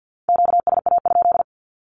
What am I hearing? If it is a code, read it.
Morse code, heard as a keyed tone, about 50 wpm, 700 Hz. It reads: QSA?
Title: Q-Codes with Spoken Meaning - 50wpm